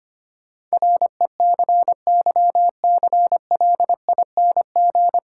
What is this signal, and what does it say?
Morse code, heard as a keyed tone, about 25 wpm, 700 Hz. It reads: RECYCLING
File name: practice / 25wpm / Difficult Sentences with Hints - 25wpm